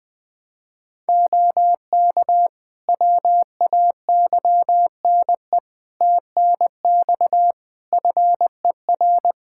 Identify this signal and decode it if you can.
Morse code, heard as a keyed tone, about 20 wpm, 700 Hz. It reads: OK WAYNE TNX FER
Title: Practice QSO - Exchange 4 - Ending - 20wpm